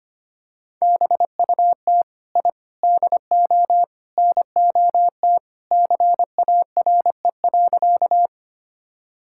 Morse code, heard as a keyed tone, about 25 wpm, 700 Hz. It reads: BUT I DO NOT CARE.
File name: practice / 25wpm / Sentences with 4-Word Phrases - 25wpm